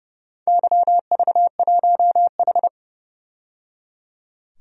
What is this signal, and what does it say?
Morse code, heard as a keyed tone, about 30 wpm, 700 Hz. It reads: YV1H
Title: DXCC-100 Call Signs - 4 Characters - 30wpm